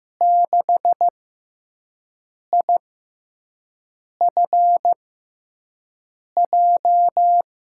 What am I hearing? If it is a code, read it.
Morse code, heard as a keyed tone, about 15 wpm, spaced as for 5 wpm, 700 Hz. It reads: 6IFJ
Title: USA Call Signs - 6 Characters - 6x Character Spacing - 15wpm